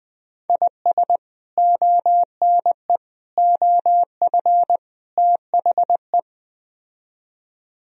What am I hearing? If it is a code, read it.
Morse code, heard as a keyed tone, about 21 wpm, 700 Hz. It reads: IS ONE OF THE
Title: Speed-Racing - Sets of 4 Words 17wpm